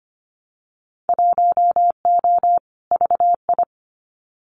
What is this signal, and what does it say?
Morse code, heard as a keyed tone, about 25 wpm, 690 Hz. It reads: 1O VI
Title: Winter Field Day Exchanges Encore 25wpm